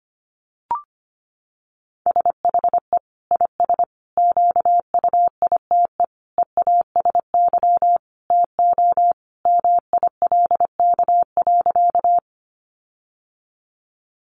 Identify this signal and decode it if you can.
Morse code, heard as a keyed tone, about 25 wpm, 700 Hz. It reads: SHE IS QUITE EASY TO MILK.
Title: Sentences from Top 1500 Words 25wpm